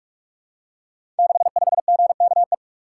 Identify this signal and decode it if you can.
Morse code, heard as a keyed tone, about 44 wpm, 700 Hz. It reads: 65GKE
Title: Code Groups - 5 Letter:Numbers - 45wpm